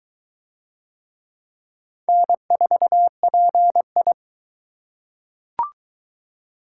Morse code, heard as a keyed tone, about 22 wpm, 700 Hz. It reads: N4PI